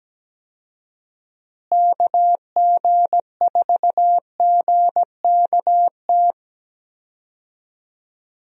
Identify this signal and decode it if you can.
Morse code, heard as a keyed tone, about 17 wpm, 700 Hz. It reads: KG4GKT
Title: USA Call Signs - 6 Characters - Repeated 2x - 17wpm